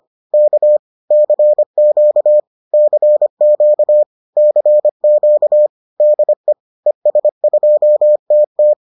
Morse code, heard as a keyed tone, about 25 wpm, 600 Hz. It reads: K CQ CQ CQ DE ES2TT